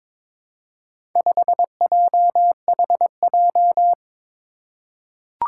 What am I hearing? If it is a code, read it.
Morse code, heard as a keyed tone, about 22 wpm, 700 Hz. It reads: 5JHJ
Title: Four Challenging Letter-Number Combinations 22wpm